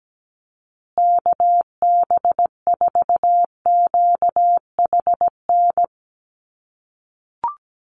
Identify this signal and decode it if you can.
Morse code, heard as a keyed tone, about 17 wpm, 700 Hz. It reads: KB4QHN
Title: USA Call Signs Repeated 3x 17wpm